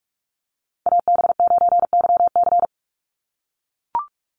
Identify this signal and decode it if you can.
Morse code, heard as a keyed tone, about 44 wpm, 700 Hz. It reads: AB9YC
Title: Call Signs - 45wpm